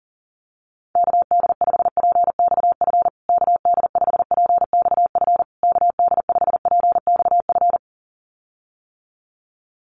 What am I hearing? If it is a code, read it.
Morse code, heard as a keyed tone, about 40 wpm, 700 Hz. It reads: KD5PXF KD5PXF KD5PXF
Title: USA Call Signs Repeated 3x 40wpm